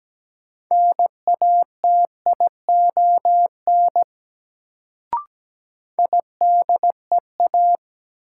Morse code, heard as a keyed tone, about 17 wpm, 700 Hz. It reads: NATION IDEA
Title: Sets of 3 Words - New Word Drill - 17wpm